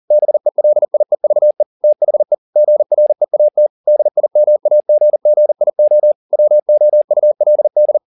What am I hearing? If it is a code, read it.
Morse code, heard as a keyed tone, about 40 wpm, 600 Hz. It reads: BELIEVE THE GREAT DIMAGGIO WOULD